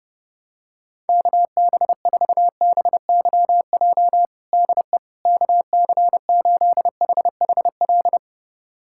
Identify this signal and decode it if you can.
Morse code, heard as a keyed tone, about 30 wpm, 700 Hz. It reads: KB4BYJ DE KC8HHL
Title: USA Callsign DE Callsign 30wpm